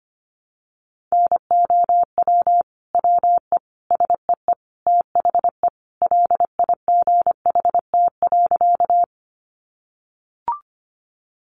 Morse code, heard as a keyed tone, about 25 wpm, 700 Hz. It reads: NOW WE SEE THE LIGHT.
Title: Sentences from Top 200 Words 25wpm